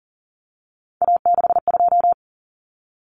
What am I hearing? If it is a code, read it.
Morse code, heard as a keyed tone, about 40 wpm, 700 Hz. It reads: A62
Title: Common Cut Numbers - 3-Digit 40wpm